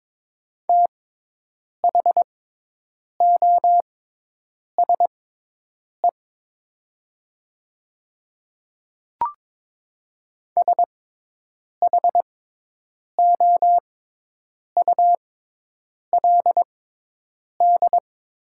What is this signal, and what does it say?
Morse code, heard as a keyed tone, about 22 wpm, spaced as for 8 wpm, 700 Hz. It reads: THOSE SHOULD
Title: 6x Character Spacing - Top 300 Words - 22wpm